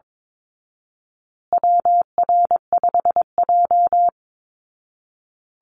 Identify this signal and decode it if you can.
Morse code, heard as a keyed tone, about 22 wpm, 700 Hz. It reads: WR5J
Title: Call Signs Repeated 2x - 22wpm